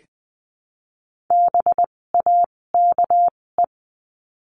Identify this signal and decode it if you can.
Morse code, heard as a keyed tone, about 20 wpm, spaced as for 16 wpm, 700 Hz. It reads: BAKE
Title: 256 Lesson 42a - Words 12-20wpm